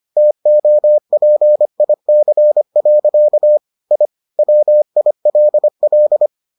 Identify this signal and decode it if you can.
Morse code, heard as a keyed tone, about 25 wpm, 600 Hz. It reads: TOPIC. I WILL